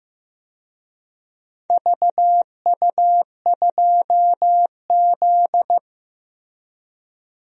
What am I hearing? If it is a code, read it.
Morse code, heard as a keyed tone, about 15 wpm, 700 Hz. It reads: VU2Z
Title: DXCC-100 Call Signs - 4 Characters - 15wpm